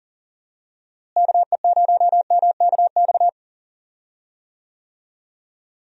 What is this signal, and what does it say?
Morse code, heard as a keyed tone, about 38 wpm, 700 Hz. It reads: KE0MKX
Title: USA Call Signs - 6 Characters - Repeated 3x - 40wpm